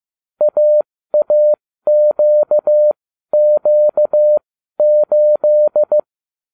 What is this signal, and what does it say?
Morse code, heard as a keyed tone, about 15 wpm, spaced as for 12 wpm, 600 Hz. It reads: AAQQ8